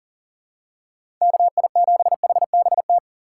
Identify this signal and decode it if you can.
Morse code, heard as a keyed tone, about 40 wpm, 700 Hz. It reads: KI7HBT
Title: USA Call Signs - 6 Characters - 3x Character Spacing - 40wpm